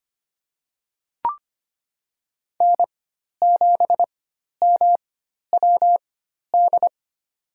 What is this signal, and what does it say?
Morse code, heard as a keyed tone, about 25 wpm, spaced as for 12 wpm, 700 Hz. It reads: N7MWD
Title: USA Call Signs - 5 Characters - 4x Character Spacing - 25wpm